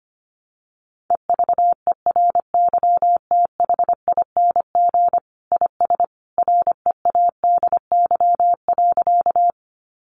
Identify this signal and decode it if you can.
Morse code, heard as a keyed tone, about 25 wpm, 700 Hz. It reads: EVERYTHING IS READY.
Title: Sentences from Top 700 Words 25wpm